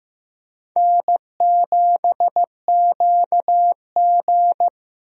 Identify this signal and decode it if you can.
Morse code, heard as a keyed tone, about 15 wpm, 700 Hz. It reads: N7QG
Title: USA Call Signs - 4 Characters - Repeated 2x - 15wpm